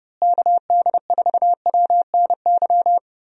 Morse code, heard as a keyed tone, about 30 wpm, 700 Hz. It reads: KD4WNY
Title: USA Call Signs - 6 Characters - Repeated 2x - 30wpm